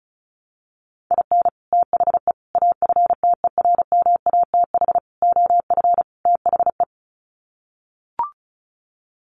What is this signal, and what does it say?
Morse code, heard as a keyed tone, about 34 wpm, 700 Hz. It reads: IN THE AFTERMATH OF THE
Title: Sets of 5 Words 35wpm